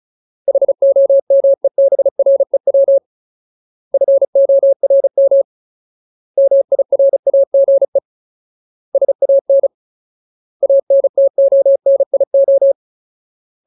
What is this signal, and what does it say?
Morse code, heard as a keyed tone, about 34 wpm, 550 Hz. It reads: HOMEBREW FORM MIRAGE SAN ANTONIO
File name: Ultimate Word List / 35WPM Ultimate